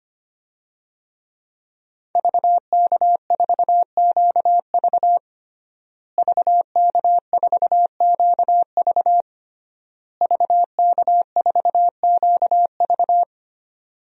Morse code, heard as a keyed tone, about 25 wpm, 700 Hz. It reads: VK4QV VK4QV VK4QV
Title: DXCC-100 Call Signs - 5 Characters - Repeated 3x - 25wpm